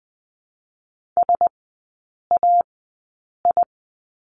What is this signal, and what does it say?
Morse code, heard as a keyed tone, about 20 wpm, spaced as for 8 wpm, 700 Hz. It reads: SAI
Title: 041 Lesson 8e - Number 1 - Three Characters 20wpm